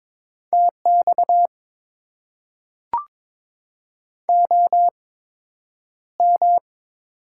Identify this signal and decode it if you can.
Morse code, heard as a keyed tone, about 22 wpm, spaced as for 9 wpm, 700 Hz. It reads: TX OM
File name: Top 90 QSO Elements - 8x Character Spacing - 22wpm